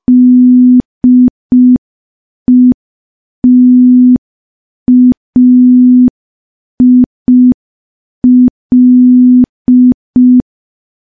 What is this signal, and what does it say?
Morse code, heard as a keyed tone, about 5 wpm, 250 Hz. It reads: DETAIL
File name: morse / morse-training-05WPM